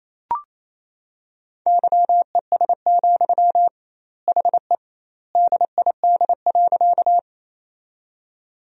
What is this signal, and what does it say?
Morse code, heard as a keyed tone, about 28 wpm, 700 Hz. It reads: YES, HE DID.
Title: 2x Word-Spacing - Sentences from Top 500 Words - 28wpm